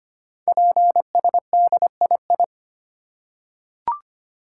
Morse code, heard as a keyed tone, about 25 wpm, 700 Hz. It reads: PSDII
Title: Code Groups - 5 Letter:Numbers - 25wpm